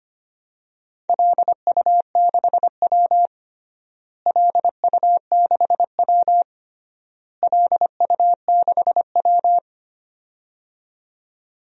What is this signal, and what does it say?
Morse code, heard as a keyed tone, about 25 wpm, 700 Hz. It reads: LU6W LU6W LU6W
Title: DXCC-100 Call Signs - 4 Characters - Repeated 3x - 25wpm